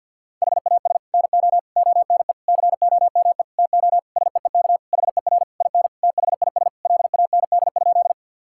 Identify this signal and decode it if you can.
Morse code, heard as a keyed tone, about 50 wpm, 700 Hz. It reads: HAS NO ONE COME TO SEEK HER IN THIS LAND?
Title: Sentences from Top 1600 Words 50wpm